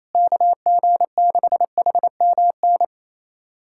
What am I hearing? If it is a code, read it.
Morse code, heard as a keyed tone, about 28 wpm, 700 Hz. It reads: KG6HMN
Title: USA Callsign DE Callsign 28wpm